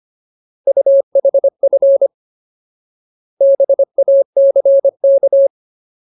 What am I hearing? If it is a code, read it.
Morse code, heard as a keyed tone, about 25 wpm, 550 Hz. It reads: UHF BACK
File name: Ultimate Word List / 25WPM Ultimate